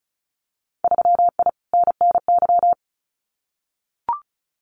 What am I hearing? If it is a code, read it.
Morse code, heard as a keyed tone, about 34 wpm, 700 Hz. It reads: 3I NNY
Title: Winter Field Day Exchanges 35wpm